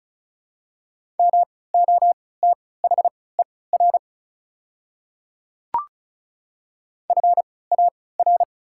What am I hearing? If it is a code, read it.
Morse code, heard as a keyed tone, about 36 wpm, spaced as for 20 wpm, 690 Hz. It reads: MOTHER FAR
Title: 3x Character Spacing - Top 300 Words - 35wpm